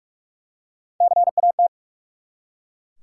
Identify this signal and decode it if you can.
Morse code, heard as a keyed tone, about 43 wpm, 700 Hz. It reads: CAT
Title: Three Letter Words - 45wpm